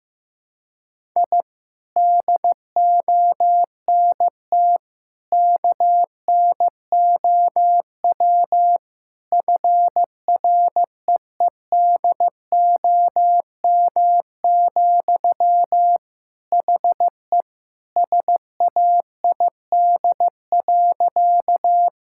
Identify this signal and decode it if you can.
Morse code, heard as a keyed tone, about 16 wpm, 700 Hz. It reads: I DONT KNOW FREEDOM, HE SAID.
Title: Sentences from Top 1000 Words - Review Part 5 - 15wpm